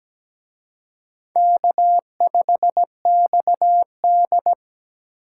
Morse code, heard as a keyed tone, about 20 wpm, 700 Hz. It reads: K5XD
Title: Speed Racing - USA Call Signs 17wpm